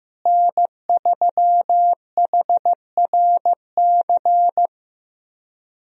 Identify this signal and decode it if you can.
Morse code, heard as a keyed tone, about 15 wpm, 700 Hz. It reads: N3HRC
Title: USA Call Signs - 5 Characters - Repeated 3x - 15wpm